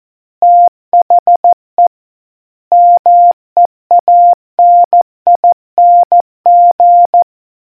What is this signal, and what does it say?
Morse code, heard as a keyed tone, about 14 wpm, 700 Hz. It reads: THE MEANING